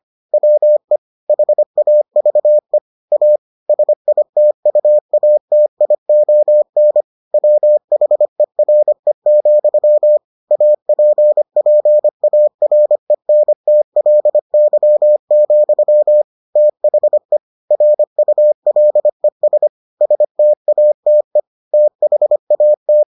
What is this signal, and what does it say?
Morse code, heard as a keyed tone, about 25 wpm, 600 Hz. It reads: WE HAVE A SITUATION WHERE, APPARENTLY, THE RULES STATE THAT